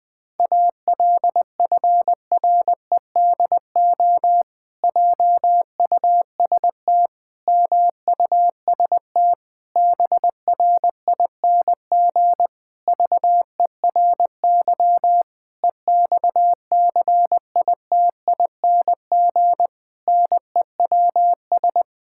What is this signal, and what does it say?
Morse code, heard as a keyed tone, about 20 wpm, 700 Hz. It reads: ALFREDO JUST MUST BRING VERY EXCITING NEWS
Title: Pangram Challenge v2 - 20wpm